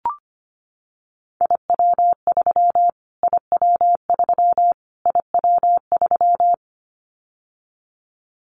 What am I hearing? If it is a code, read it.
Morse code, heard as a keyed tone, about 25 wpm, 700 Hz. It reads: IW3 IW3 IW3
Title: Callsign Prefixes for Top 25 DXCC - Repeated 3x - 25wpm